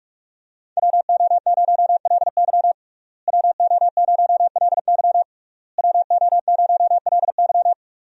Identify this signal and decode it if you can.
Morse code, heard as a keyed tone, about 46 wpm, 700 Hz. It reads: WO0LY WO0LY WO0LY
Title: USA Call Signs - 5 Characters - Repeated 3x - 45wpm